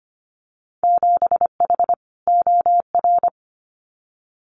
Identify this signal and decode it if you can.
Morse code, heard as a keyed tone, about 25 wpm, 700 Hz. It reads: 7H OR
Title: Winter Field Day Exchanges Encore 25wpm